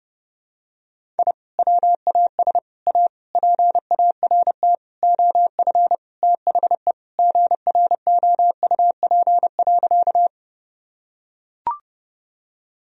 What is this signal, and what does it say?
Morse code, heard as a keyed tone, about 30 wpm, 700 Hz. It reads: I WAS A PART OF THE GROUP.
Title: Sentences from Top 1000 Words - Review Part 1 - 30wpm